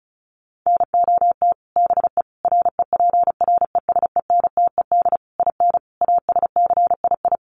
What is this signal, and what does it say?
Morse code, heard as a keyed tone, about 34 wpm, 700 Hz. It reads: NOT BE REPRESENTED IN ASCII
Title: Sets of 5 Words 35wpm